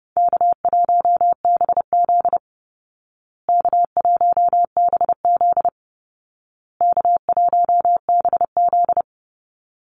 Morse code, heard as a keyed tone, about 30 wpm, 700 Hz. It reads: K1BZ K1BZ K1BZ
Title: Easy Call Signs Repeated 3x - 30wpm